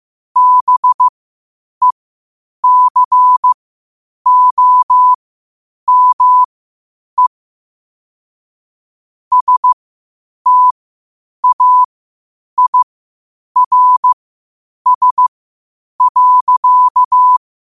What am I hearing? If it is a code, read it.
Morse code, heard as a keyed tone, about 15 wpm, spaced as for 8 wpm, 1000 Hz. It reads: BECOME STAIRS.
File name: qotdfiles / 8wpm